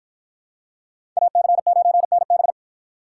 Morse code, heard as a keyed tone, about 53 wpm, 700 Hz. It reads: AC9NB